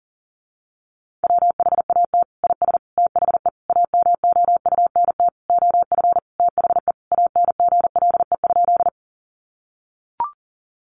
Morse code, heard as a keyed tone, about 40 wpm, 700 Hz. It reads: WHAT IS THE AMOUNT OF THE ANGLE?
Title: Sentences from Top 1500 Words 40wpm